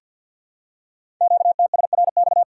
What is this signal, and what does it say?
Morse code, heard as a keyed tone, about 47 wpm, 700 Hz. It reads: QTSRX
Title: Code Groups - 5 Letters - 50wpm